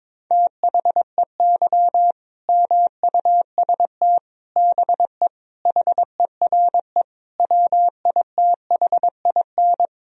Morse code, heard as a keyed tone, about 22 wpm, 700 Hz. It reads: THEY MUST BE HERE WITHIN
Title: Sentences from Top 1000 Words - Review Part 4 - 22wpm